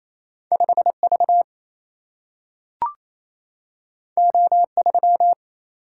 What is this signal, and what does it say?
Morse code, heard as a keyed tone, about 28 wpm, 700 Hz. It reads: 5 V O 3
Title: Single-Letter Single-Number 28wpm